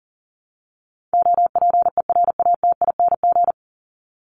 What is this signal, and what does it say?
Morse code, heard as a keyed tone, about 37 wpm, 700 Hz. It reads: OPERATING
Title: ING Suffix - Top 100 Words with Base Words 40wpm